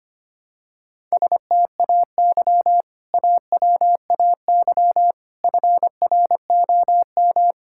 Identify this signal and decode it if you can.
Morse code, heard as a keyed tone, about 25 wpm, 700 Hz. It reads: STAY AWAY FROM
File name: practice / 25wpm / Sentences with 2-Word Phrases - 25wpm